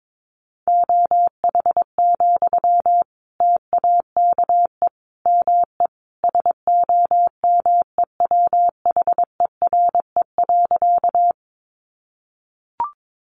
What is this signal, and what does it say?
Morse code, heard as a keyed tone, about 22 wpm, 700 Hz. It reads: OH, TAKE ME SOMEWHERE.
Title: Sentences from Top 1400 Words 22wpm